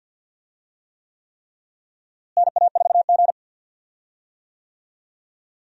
Morse code, heard as a keyed tone, about 50 wpm, 700 Hz. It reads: NA4G
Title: USA Call Signs - 4 Characters - Repeated 3x - 50wpm